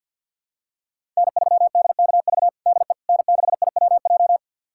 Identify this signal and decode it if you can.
Morse code, heard as a keyed tone, about 51 wpm, 700 Hz. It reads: N3DKV DE N6IWJ
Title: USA Callsign DE Callsign - 5 Characters - 50wpm